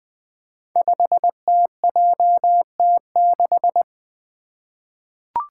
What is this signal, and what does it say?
Morse code, heard as a keyed tone, about 21 wpm, 700 Hz. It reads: 5TJT6
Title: Five Challenging Letter-Number Combinations 20wpm